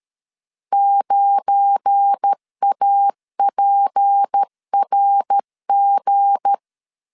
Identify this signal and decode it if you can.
Morse code, heard as a keyed tone, about 13 wpm, 800 Hz. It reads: 9APRG
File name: LEZIONI / 7 GF9Z / 7 GF9 12wpm 2319